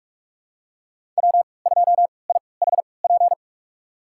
Morse code, heard as a keyed tone, about 46 wpm, spaced as for 26 wpm, 700 Hz. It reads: W2IHP
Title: USA Call Signs - 5 Characters - 3x Character Spacing - 45wpm